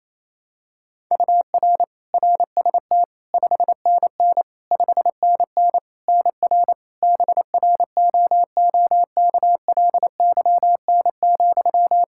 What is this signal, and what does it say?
Morse code, heard as a keyed tone, about 27 wpm, 700 Hz. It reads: UR RST 5NN 5NN NR BROOKLYN,
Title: K1USN SST QSO Practice 28wpm